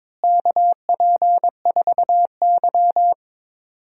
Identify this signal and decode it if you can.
Morse code, heard as a keyed tone, about 22 wpm, 700 Hz. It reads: KP4Y
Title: DXCC-100 Call Signs - 4 Characters - Repeated 3x - 22wpm